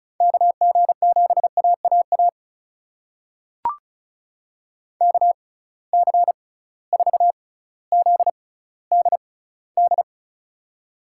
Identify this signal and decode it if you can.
Morse code, heard as a keyed tone, about 36 wpm, spaced as for 18 wpm, 700 Hz. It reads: KG7AAA KC4ZDD
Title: USA Call Signs - 6 Characters - 6x Character Spacing - 35wpm